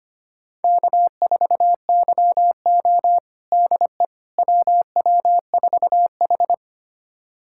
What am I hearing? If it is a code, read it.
Morse code, heard as a keyed tone, about 25 wpm, 700 Hz. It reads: K4YO DE WW4H
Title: USA Callsign DE Callsign - 4 Characters - 25wpm